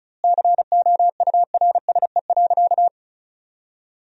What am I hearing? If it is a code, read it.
Morse code, heard as a keyed tone, about 34 wpm, 700 Hz. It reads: COURSE.
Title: 2x Word-Spacing - Sentences from Top 500 Words - 35wpm